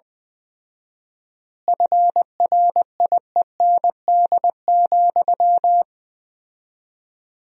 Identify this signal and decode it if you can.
Morse code, heard as a keyed tone, about 20 wpm, 700 Hz. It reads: FRIEND,
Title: 4x Word-Spacing - Sentences from Top 500 Words - 20wpm